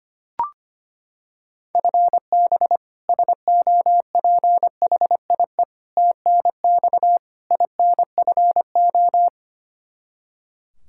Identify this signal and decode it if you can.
Morse code, heard as a keyed tone, about 25 wpm, 700 Hz. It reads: FB SOPHIE TNX INFO
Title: Practice QSO - Exchange 3 - More Info - 25wpm